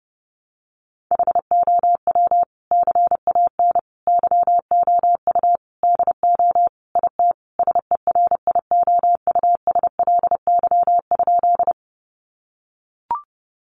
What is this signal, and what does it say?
Morse code, heard as a keyed tone, about 30 wpm, 700 Hz. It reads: HOW CAN YOU DO IT SERIOUSLY?
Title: Sentences from Top 1200 Words 30wpm